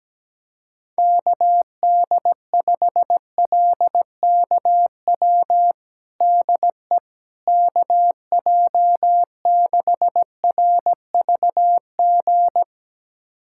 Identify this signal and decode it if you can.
Morse code, heard as a keyed tone, about 17 wpm, 700 Hz. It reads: KD5LKW DE KJ6RVG
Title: USA Callsign DE Callsign - 6 Characters - 17wpm